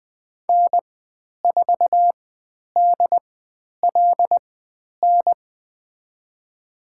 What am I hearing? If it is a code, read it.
Morse code, heard as a keyed tone, about 20 wpm, spaced as for 10 wpm, 700 Hz. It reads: N4DLN
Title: 095 Lesson 16g - Letter c - Call Signs 20wpm